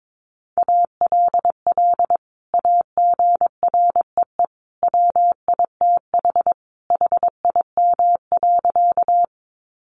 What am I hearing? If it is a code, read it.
Morse code, heard as a keyed tone, about 22 wpm, 700 Hz. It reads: ALL AGREE WITH HIM.
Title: Sentences from Top 1100 Words 22wpm